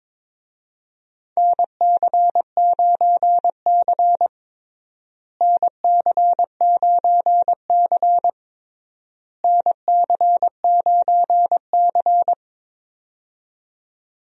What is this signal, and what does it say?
Morse code, heard as a keyed tone, about 22 wpm, 700 Hz. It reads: NC9C NC9C NC9C
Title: USA Call Signs - 4 Characters - Repeated 3x - 22wpm